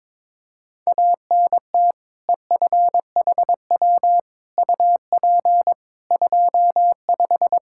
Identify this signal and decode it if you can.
Morse code, heard as a keyed tone, about 22 wpm, 700 Hz. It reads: ANT EFHW UP 25
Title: Practice QSO - Exchange 2 - Info - 22wpm